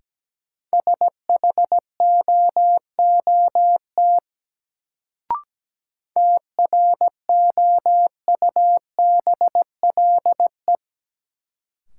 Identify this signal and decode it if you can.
Morse code, heard as a keyed tone, about 17 wpm, 700 Hz. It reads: SHOOT TROUBLE
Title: Sentences from Top 1200 Words - New Word Drill - 17wpm